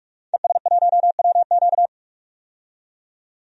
Morse code, heard as a keyed tone, about 46 wpm, 700 Hz. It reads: ES1WQ
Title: DXCC-100 Call Signs - 5 Characters - 45wpm